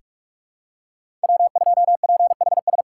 Morse code, heard as a keyed tone, about 44 wpm, 700 Hz. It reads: W2PHS
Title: USA Callsign DE Callsign - 5 Characters - 45wpm